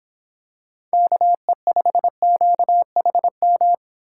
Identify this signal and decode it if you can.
Morse code, heard as a keyed tone, about 26 wpm, 700 Hz. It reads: KE5QHM